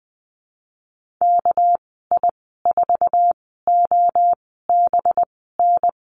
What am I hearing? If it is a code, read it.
Morse code, heard as a keyed tone, about 20 wpm, spaced as for 14 wpm, 700 Hz. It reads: KI4OBN
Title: 219 Lesson 35g - Number 0 - Call Signs 20wpm